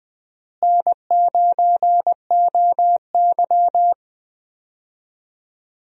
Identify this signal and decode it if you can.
Morse code, heard as a keyed tone, about 20 wpm, 700 Hz. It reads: N9OY
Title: USA Call Signs - 4 Characters - Repeated 2x - 20wpm